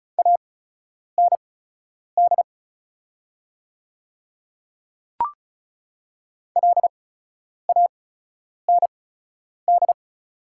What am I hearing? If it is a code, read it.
Morse code, heard as a keyed tone, about 36 wpm, spaced as for 10 wpm, 690 Hz. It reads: AND LAND
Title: 8x Character Spacing - Top 500 Words - 35wpm